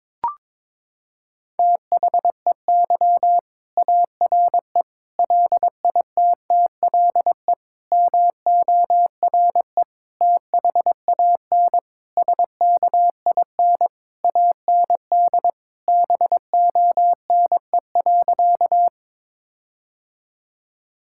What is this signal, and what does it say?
Morse code, heard as a keyed tone, about 22 wpm, 700 Hz. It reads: THEY ARE LITTLE MORE THAN SKIN AND BONE.
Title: Sentences from Top 1600 Words 22wpm